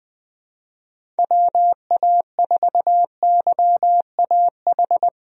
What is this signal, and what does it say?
Morse code, heard as a keyed tone, about 20 wpm, 700 Hz. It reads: WA4YAH
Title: USA Call Signs - 6 Characters - Repeated 2x - 20wpm